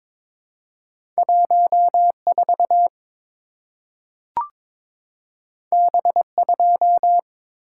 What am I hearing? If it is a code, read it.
Morse code, heard as a keyed tone, about 22 wpm, 700 Hz. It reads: 14 B2
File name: practice / 22wpm / Two Letter-Number Combinations - 22wpm